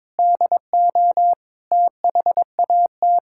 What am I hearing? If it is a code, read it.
Morse code, heard as a keyed tone, about 21 wpm, 700 Hz. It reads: DO THAT
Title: Sentences from Top 1000 Words - Review Part 6 - 22wpm